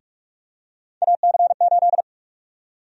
Morse code, heard as a keyed tone, about 42 wpm, 700 Hz. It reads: AC8
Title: USA Call Sign Prefixes - 45wpm